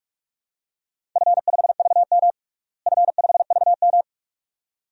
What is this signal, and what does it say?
Morse code, heard as a keyed tone, about 46 wpm, 700 Hz. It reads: F5VM F5VM
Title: DXCC-100 Call Signs - 4 Characters - Repeated 2x - 45wpm